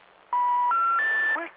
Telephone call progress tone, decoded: special information tone